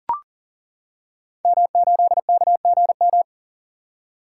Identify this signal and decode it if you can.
Morse code, heard as a keyed tone, about 39 wpm, 700 Hz. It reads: M8KGM
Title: Code Groups - 5 Letter:Numbers - 40wpm